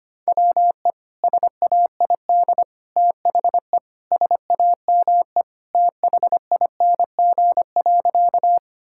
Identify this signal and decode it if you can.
Morse code, heard as a keyed tone, about 25 wpm, 700 Hz. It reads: WE SAID THE SAME THING.
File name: practice / 25wpm / Sentences with 3-Word Phrases - 25wpm